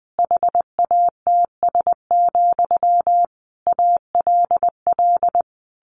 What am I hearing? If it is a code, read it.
Morse code, heard as a keyed tone, about 20 wpm, 700 Hz. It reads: HATS, ALL
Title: A Sound Of Thunder 20wpm